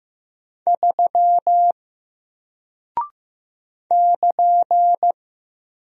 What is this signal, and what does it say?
Morse code, heard as a keyed tone, about 15 wpm, 700 Hz. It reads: 3 <KN>
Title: Single Character (Letter:Number:Punctuation:Prosign) Alt 15wpm